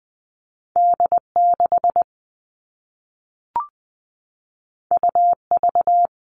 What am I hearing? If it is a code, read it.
Morse code, heard as a keyed tone, about 20 wpm, 700 Hz. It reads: D6 UV
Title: All Cut Numbers - 2-Digit 20wpm